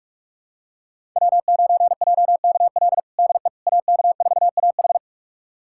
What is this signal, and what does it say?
Morse code, heard as a keyed tone, about 46 wpm, 700 Hz. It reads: W9JKL DE AK4AH
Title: USA Callsign DE Callsign 45wpm